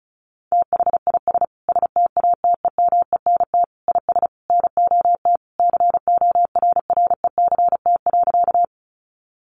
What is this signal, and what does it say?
Morse code, heard as a keyed tone, about 34 wpm, 700 Hz. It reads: THIS STATEMENT IS NOT CORRECT.
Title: Sentences from Top 1200 Words 35wpm